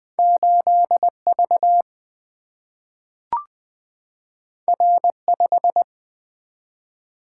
Morse code, heard as a keyed tone, about 20 wpm, 700 Hz. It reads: 8 V R 5
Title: Single-Letter Single-Number 20wpm